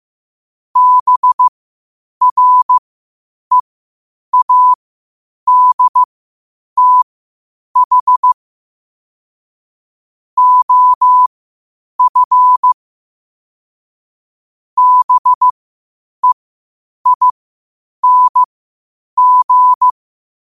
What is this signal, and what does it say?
Morse code, heard as a keyed tone, about 15 wpm, spaced as for 8 wpm, 1000 Hz. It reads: BREADTH OF BEING
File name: qotdfiles / 8wpm